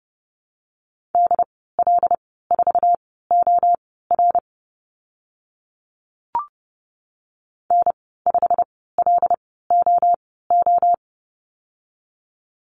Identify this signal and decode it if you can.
Morse code, heard as a keyed tone, about 30 wpm, spaced as for 17 wpm, 690 Hz. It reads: DL4OR N5LOO
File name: Lesson 15g - Number 5 - Call Signs 30wpm